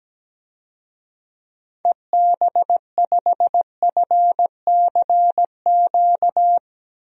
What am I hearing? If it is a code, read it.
Morse code, heard as a keyed tone, about 17 wpm, 700 Hz. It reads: EB5FCQ
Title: DXCC-100 Call Signs - 6 Characters - Repeated 2x - 17wpm